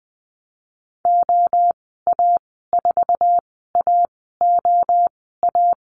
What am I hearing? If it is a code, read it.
Morse code, heard as a keyed tone, about 20 wpm, spaced as for 14 wpm, 700 Hz. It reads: OA4AOA
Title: 048 Lesson 9g - Number 4 - Call Signs 20wpm